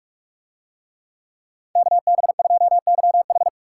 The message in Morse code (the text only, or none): KB2YH